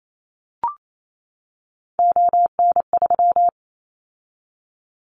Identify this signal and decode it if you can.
Morse code, heard as a keyed tone, about 28 wpm, 700 Hz. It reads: ON3
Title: Callsign Prefixes for Top 100 DXCC - 28wpm